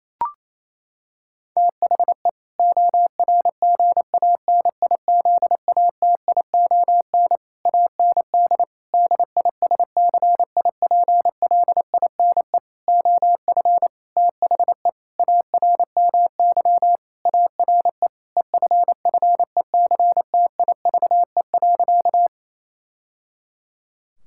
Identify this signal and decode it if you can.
Morse code, heard as a keyed tone, about 28 wpm, 700 Hz. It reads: THE ORGANIZATION AND DISCIPLINE OF THE ARMY ARE EFFECTIVE.
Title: Sentences from Top 1000 Words - Review Part 6 - 28wpm